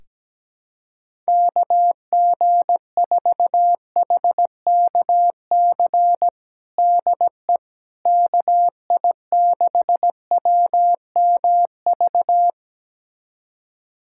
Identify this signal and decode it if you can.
Morse code, heard as a keyed tone, about 17 wpm, 700 Hz. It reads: KG4HKC DE KI6WMV
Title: USA Callsign DE Callsign 17wpm